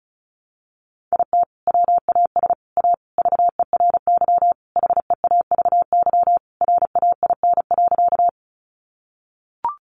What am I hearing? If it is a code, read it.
Morse code, heard as a keyed tone, about 34 wpm, 700 Hz. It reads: IT WAS A VERY HEAVY RAIN.
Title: Sentences from Top 1100 Words 35wpm